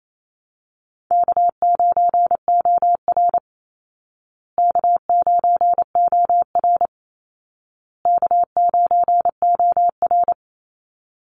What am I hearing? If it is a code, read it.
Morse code, heard as a keyed tone, about 28 wpm, 700 Hz. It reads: K9OR K9OR K9OR
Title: Call Signs Repeated 2x - 28wpm